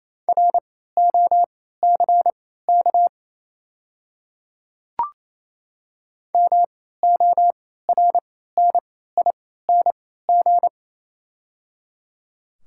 Morse code, heard as a keyed tone, about 28 wpm, spaced as for 16 wpm, 700 Hz. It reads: ROCK MORNING
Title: 3x Character Spacing - Top 500 Words - 28wpm